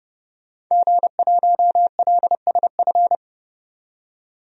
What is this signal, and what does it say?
Morse code, heard as a keyed tone, about 30 wpm, 700 Hz. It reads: G1LSF
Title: DXCC-100 Call Signs - 5 Characters - 30wpm